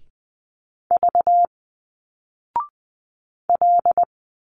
Morse code, heard as a keyed tone, about 20 wpm, spaced as for 4 wpm, 700 Hz. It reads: VL
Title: Single Letters 20wpm